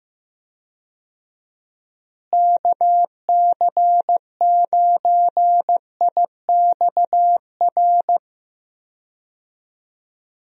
Morse code, heard as a keyed tone, about 15 wpm, 700 Hz. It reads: KC9IXR
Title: USA Call Signs - 6 Characters - Repeated 3x - 15wpm